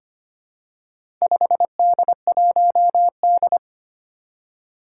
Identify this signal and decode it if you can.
Morse code, heard as a keyed tone, about 25 wpm, 700 Hz. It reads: 5D1D
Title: Four Challenging Letter-Number Combinations 25wpm